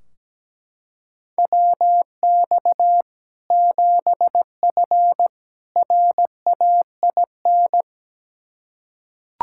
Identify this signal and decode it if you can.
Morse code, heard as a keyed tone, about 17 wpm, 700 Hz. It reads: WX 7F RAIN
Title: K1USN SST QSO Practice 17wpm